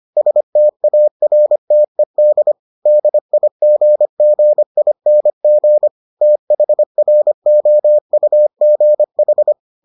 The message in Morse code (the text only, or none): STARTED DIGGING THROUGH